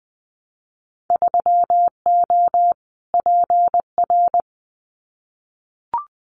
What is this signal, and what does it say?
Morse code, heard as a keyed tone, about 20 wpm, 700 Hz. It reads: 3O PR
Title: Winter Field Day Exchanges Encore 20wpm